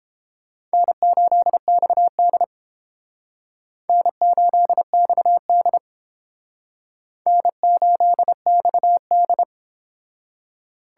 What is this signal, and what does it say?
Morse code, heard as a keyed tone, about 30 wpm, 700 Hz. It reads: N8XD N8XD N8XD